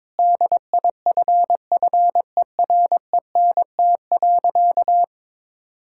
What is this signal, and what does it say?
Morse code, heard as a keyed tone, about 21 wpm, 700 Hz. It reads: DIFFERENT.
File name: practice / 22wpm / Sentences with 4-Word Phrases - 22wpm